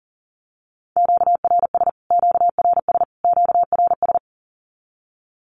Morse code, heard as a keyed tone, about 40 wpm, 700 Hz. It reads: QRS QRS QRS
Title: Q-Codes Repeated 3x 40wpm